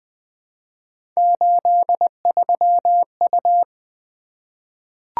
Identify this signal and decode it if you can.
Morse code, heard as a keyed tone, about 20 wpm, 700 Hz. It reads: 83U
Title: Three Letter-Number Combinations - 20wpm